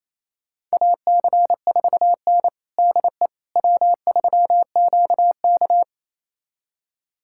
Morse code, heard as a keyed tone, about 28 wpm, 700 Hz. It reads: AC4N DE W3QK
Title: USA Callsign DE Callsign - 4 Characters - 28wpm